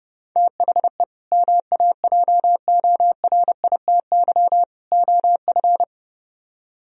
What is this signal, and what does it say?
Morse code, heard as a keyed tone, about 37 wpm, 700 Hz. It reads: THE MAJORITY OF
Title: Speed-Racing - Sets of 3 Words 30wpm